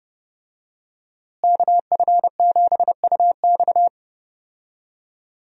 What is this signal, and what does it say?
Morse code, heard as a keyed tone, about 31 wpm, 700 Hz. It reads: KF7UX
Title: USA Call Signs - 5 Characters - Repeated 3x - 30wpm